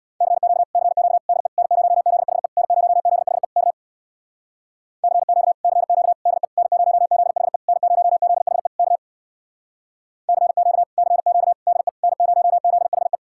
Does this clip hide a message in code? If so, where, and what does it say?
Yes: Morse code from 0.2 to 13.2 s, heard as a keyed tone, about 64 wpm, 700 Hz. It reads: CQ CQ DE N0ZHE N0ZHE K CQ CQ DE N0ZHE N0ZHE K CQ CQ DE N0ZHE